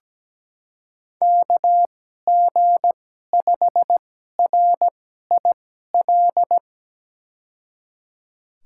Morse code, heard as a keyed tone, about 17 wpm, spaced as for 12 wpm, 700 Hz. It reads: KG5RIL
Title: USA Call Signs - 6 Characters - 2x Character Spacing - 17wpm